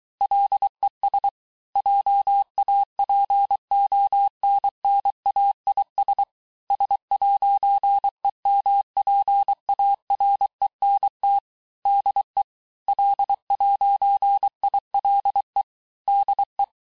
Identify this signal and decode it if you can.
Morse code, heard as a keyed tone, about 23 wpm, 800 Hz. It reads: LES JAPONNAIS S'EMPARENT DE L'ILE DE
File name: texte-clair-vitesse-17mpm-puis-22mpm